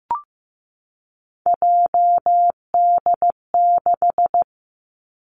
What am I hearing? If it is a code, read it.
Morse code, heard as a keyed tone, about 15 wpm, 700 Hz. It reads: JD6
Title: Three Challenging Letter-Number Combinations 15wpm